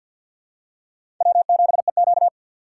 Morse code, heard as a keyed tone, about 47 wpm, 700 Hz. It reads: W7EQ
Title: USA Call Signs - 4 Characters - 3x Character Spacing - 50wpm